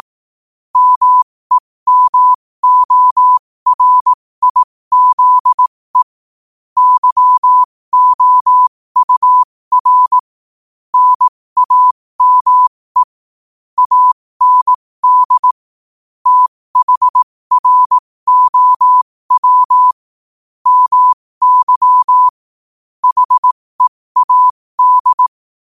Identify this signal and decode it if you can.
Morse code, heard as a keyed tone, about 18 wpm, spaced as for 15 wpm, 1000 Hz. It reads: MEMORIZE YOUR NAME AND THROW MY HEAD
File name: qotdfiles / 15wpm